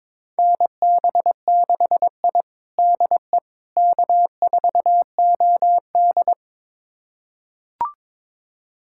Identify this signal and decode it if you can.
Morse code, heard as a keyed tone, about 22 wpm, 700 Hz. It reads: NB6I DE K4OD
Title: USA Callsign DE Callsign - 4 Characters - 22wpm